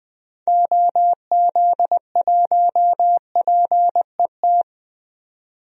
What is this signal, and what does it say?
Morse code, heard as a keyed tone, about 20 wpm, 700 Hz. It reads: OZ1PET
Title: DXCC-100 Call Signs - 6 Characters - 20wpm